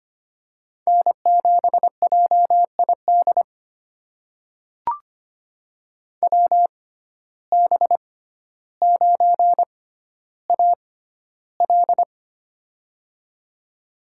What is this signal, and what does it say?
Morse code, heard as a keyed tone, about 25 wpm, spaced as for 13 wpm, 700 Hz. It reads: N7JID WB9AL
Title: USA Call Signs - 6x Character Spacing - 25wpm